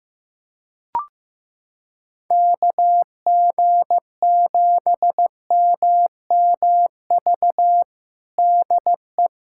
Morse code, heard as a keyed tone, about 15 wpm, 700 Hz. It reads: KG7MMV DE
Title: USA Callsign DE Callsign 15wpm